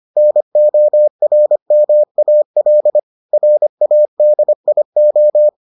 Morse code, heard as a keyed tone, about 25 wpm, 600 Hz. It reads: NORMAL RADIO